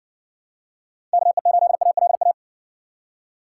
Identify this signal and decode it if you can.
Morse code, heard as a keyed tone, about 60 wpm, 700 Hz. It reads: KE8ALA